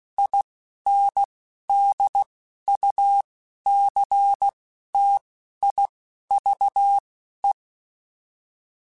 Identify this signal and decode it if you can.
Morse code, heard as a keyed tone, about 16 wpm, spaced as for 11 wpm, 800 Hz. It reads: INDUCTIVE
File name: Technical Subjects / Technical Subjects-mp3s / XL @10